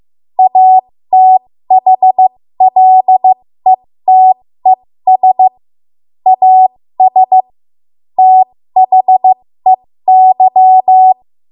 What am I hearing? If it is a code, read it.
Morse code, heard as a keyed tone, about 15 wpm, spaced as for 13 wpm, 750 Hz. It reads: ATHLETES AS THEY
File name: Morse / 13WPM